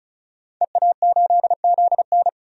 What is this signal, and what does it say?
Morse code, heard as a keyed tone, about 36 wpm, 700 Hz. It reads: EA8ZN
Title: DXCC-100 Call Signs - 5 Characters - 35wpm